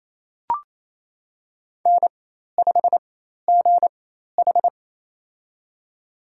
Morse code, heard as a keyed tone, about 28 wpm, spaced as for 13 wpm, 700 Hz. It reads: N5GH
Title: USA Call Signs - 4x Character Spacing - 28wpm